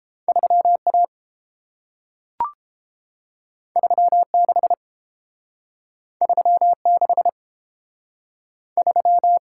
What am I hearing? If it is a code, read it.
Morse code, heard as a keyed tone, about 30 wpm, 700 Hz. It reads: 3A 36 36 3